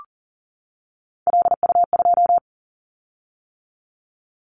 Morse code, heard as a keyed tone, about 40 wpm, 700 Hz. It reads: LU2
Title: Callsign Prefixes for Top 50 DXCC - 40wpm